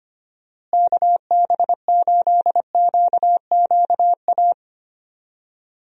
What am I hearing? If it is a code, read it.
Morse code, heard as a keyed tone, about 25 wpm, 700 Hz. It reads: KB8QQA
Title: USA Call Signs - 6 Characters - Repeated 2x - 25wpm